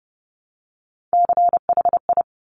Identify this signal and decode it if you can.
Morse code, heard as a keyed tone, about 30 wpm, 700 Hz. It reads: CHI
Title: Top Three Letter Combinations in Words 30wpm